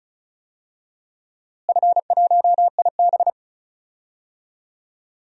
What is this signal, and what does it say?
Morse code, heard as a keyed tone, about 34 wpm, 700 Hz. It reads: F1IB
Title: DXCC-100 Call Signs - 4 Characters - Repeated 3x - 35wpm